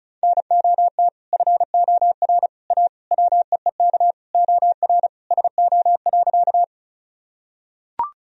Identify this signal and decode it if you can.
Morse code, heard as a keyed tone, about 36 wpm, 700 Hz. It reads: NOT FOR A WEEK OR SO.
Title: Sentences from Top 1000 Words - Review Part 2 - 35wpm